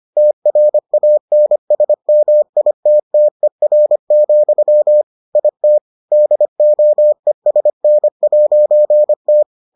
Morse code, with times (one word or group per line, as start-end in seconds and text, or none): 0.2-5.0 TRANSMITTER,
5.4-5.8 IT
6.1-9.4 DOESN'T